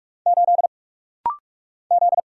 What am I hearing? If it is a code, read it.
Morse code, heard as a keyed tone, about 44 wpm, 700 Hz. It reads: 8 Z
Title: Single Letter-Number Alt - Rapid-Fire 45wpm